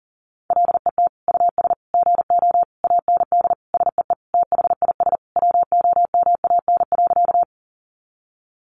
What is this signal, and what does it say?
Morse code, heard as a keyed tone, about 40 wpm, 700 Hz. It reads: LET US GO AND SEE THIS WOMAN.
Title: Sentences from Top 1100 Words 40wpm